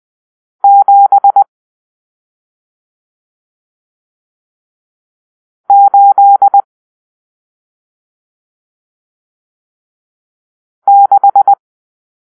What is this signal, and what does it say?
Morse code, heard as a keyed tone, about 20 wpm, 800 Hz. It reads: 7 8 6
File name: arrl / Grp-7 Drill-1